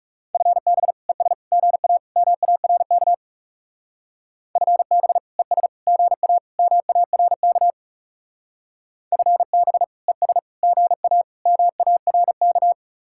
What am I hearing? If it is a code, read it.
Morse code, heard as a keyed tone, about 39 wpm, 700 Hz. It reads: FB ES GA MARK FB ES GA MARK FB ES GA MARK